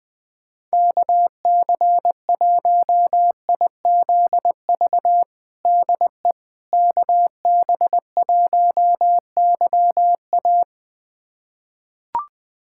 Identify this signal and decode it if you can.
Morse code, heard as a keyed tone, about 20 wpm, 700 Hz. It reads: KC1IZV DE KB1YA
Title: USA Callsign DE Callsign 20wpm